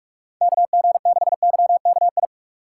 Morse code, heard as a keyed tone, about 44 wpm, 700 Hz. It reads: KG6YKI